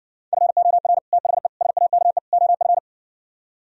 Morse code, heard as a keyed tone, about 60 wpm, 700 Hz. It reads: FOR THE SAKE OF